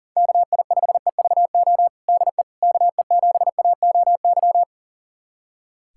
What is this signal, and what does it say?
Morse code, heard as a keyed tone, about 40 wpm, 700 Hz. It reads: KI5EVO DE KE7AOY
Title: USA Callsign DE Callsign 40wpm